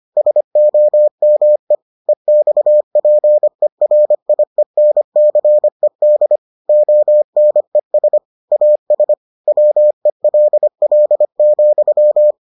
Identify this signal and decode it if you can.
Morse code, heard as a keyed tone, about 25 wpm, 600 Hz. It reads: SOME EXPERIENCED ONES AS WELL,